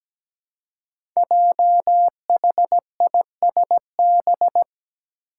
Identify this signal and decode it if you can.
Morse code, heard as a keyed tone, about 17 wpm, 700 Hz. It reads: JHISB
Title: Five Challenging Letter-Number Combinations 17wpm